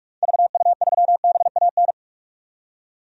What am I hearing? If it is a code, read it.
Morse code, heard as a keyed tone, about 46 wpm, 700 Hz. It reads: VU3BAN
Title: DXCC-100 Call Signs - 6 Characters - 45wpm